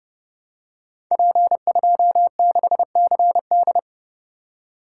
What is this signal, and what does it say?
Morse code, heard as a keyed tone, about 31 wpm, 700 Hz. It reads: P26CD
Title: Code Groups - 5 Letter:Numbers - 30wpm